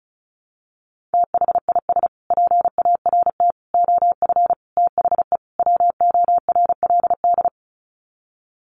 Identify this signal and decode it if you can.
Morse code, heard as a keyed tone, about 34 wpm, 700 Hz. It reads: THIS PART OF THE WORLD
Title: Sets of 5 Words Encore2 35wpm